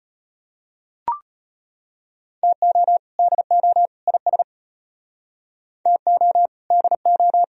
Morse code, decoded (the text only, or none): TO DO IS TO DO